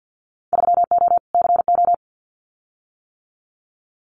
Morse code, heard as a keyed tone, about 50 wpm, 700 Hz. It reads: 3O CO
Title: Winter Field Day Exchanges Encore 50wpm